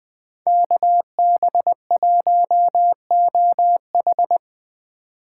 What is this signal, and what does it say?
Morse code, heard as a keyed tone, about 20 wpm, 700 Hz. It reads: KB1OH